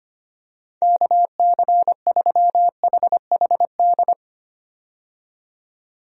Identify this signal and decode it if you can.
Morse code, heard as a keyed tone, about 25 wpm, 700 Hz. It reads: KC3HHD
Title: USA Call Signs - 6 Characters - 25wpm